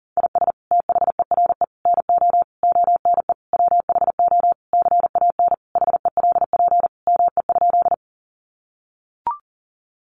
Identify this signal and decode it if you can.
Morse code, heard as a keyed tone, about 40 wpm, 700 Hz. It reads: IS THERE NO ONE WHO CAN HELP ME?
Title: Sentences from Top 200 Words 40wpm